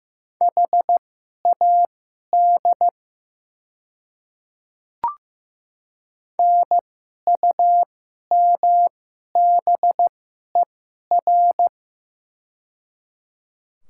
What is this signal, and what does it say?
Morse code, heard as a keyed tone, about 15 wpm, spaced as for 11 wpm, 700 Hz. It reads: HAD NUMBER
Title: 2x Character Spacing - Top 100 Words - 15wpm